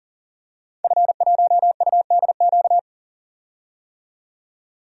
Morse code, heard as a keyed tone, about 40 wpm, 700 Hz. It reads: F1UDQ
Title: DXCC-100 Call Signs - 5 Characters - 40wpm